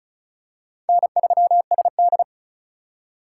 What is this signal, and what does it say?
Morse code, heard as a keyed tone, about 36 wpm, 700 Hz. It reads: N3SD
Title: USA Call Signs - 4 Characters - 3x Character Spacing - 35wpm